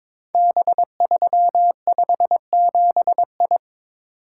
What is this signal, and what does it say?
Morse code, heard as a keyed tone, about 22 wpm, 700 Hz. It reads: B357I
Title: Code Groups - 5 Letter:Numbers - 22wpm